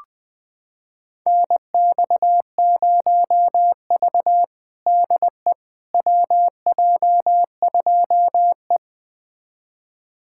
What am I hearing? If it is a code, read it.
Morse code, heard as a keyed tone, about 20 wpm, 700 Hz. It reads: NX0V DE WJ2E
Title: USA Callsign DE Callsign - 4 Characters - 20wpm